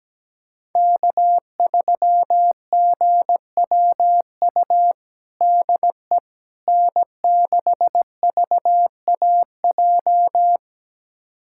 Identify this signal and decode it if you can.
Morse code, heard as a keyed tone, about 17 wpm, 700 Hz. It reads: K3GWU DE N6VAJ
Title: USA Callsign DE Callsign - 5 Characters - 17wpm